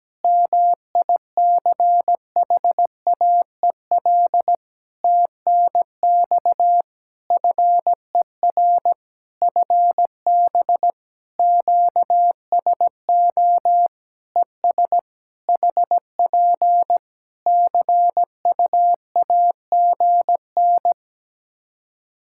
Morse code, heard as a keyed tone, about 17 wpm, 700 Hz. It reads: MICHAEL TNX FER FB QSO ES HP CUAGN
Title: K1USN SST QSO Practice 17wpm